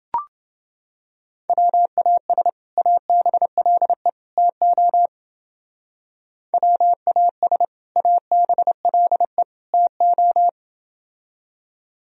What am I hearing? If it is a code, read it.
Morse code, heard as a keyed tone, about 28 wpm, 700 Hz. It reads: WAS ABLE TO WAS ABLE TO